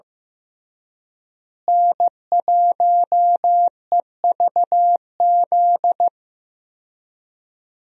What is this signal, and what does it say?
Morse code, heard as a keyed tone, about 16 wpm, 700 Hz. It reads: N1EVZ
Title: USA Call Signs - 5 Characters - Repeated 3x - 15wpm